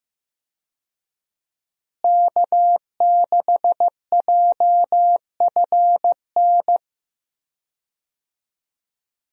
Morse code, heard as a keyed tone, about 15 wpm, 700 Hz. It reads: K6JFN
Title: USA Call Signs - 5 Characters - Repeated 2x - 15wpm